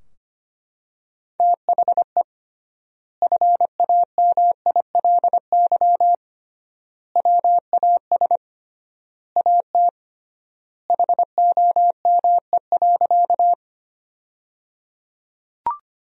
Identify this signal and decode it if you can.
Morse code, heard as a keyed tone, about 25 wpm, 700 Hz. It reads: THE FAMILY WAS AT HOME.
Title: 3x Word-Spacing - Sentences from Top 400 Words - 25wpm